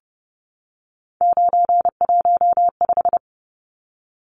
Morse code, heard as a keyed tone, about 29 wpm, 700 Hz. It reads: 915
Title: Numbers - 3-Digit 30wpm